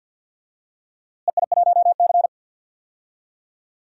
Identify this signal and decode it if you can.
Morse code, heard as a keyed tone, about 53 wpm, 700 Hz. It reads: EI1C
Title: DXCC-100 Call Signs - 4 Characters - Repeated 2x - 50wpm